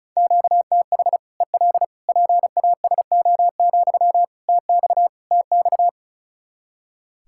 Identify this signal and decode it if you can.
Morse code, heard as a keyed tone, about 36 wpm, 700 Hz. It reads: QTH EL PASO, TX TX
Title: Practice QSO - Exchange 1 - Beginning - 35wpm